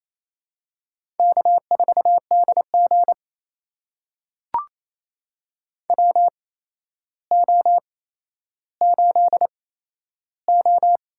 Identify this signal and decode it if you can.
Morse code, heard as a keyed tone, about 28 wpm, spaced as for 12 wpm, 700 Hz. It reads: K4DG WO8O
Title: USA Call Signs - 4 Characters - 8x Character Spacing - 28wpm